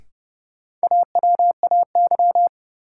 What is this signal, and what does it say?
Morse code, heard as a keyed tone, about 32 wpm, 700 Hz. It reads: AWAY
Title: Sets of 2 Words - New Word Drill - 30wpm